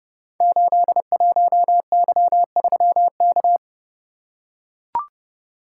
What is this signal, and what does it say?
Morse code, heard as a keyed tone, about 30 wpm, 700 Hz. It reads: 81Y3K
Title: Code Groups - 5 Letter:Numbers - 30wpm